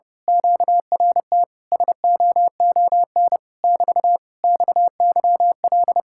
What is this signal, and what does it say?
Morse code, heard as a keyed tone, about 29 wpm, 700 Hz. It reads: QRT SOON <BT> XYL
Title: Speed-Racing Common QSO Phrases - 30wpm